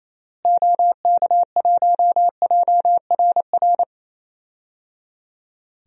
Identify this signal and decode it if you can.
Morse code, heard as a keyed tone, about 28 wpm, 700 Hz. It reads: OK1JRR
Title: DXCC-100 Call Signs - 6 Characters - Repeated 3x - 28wpm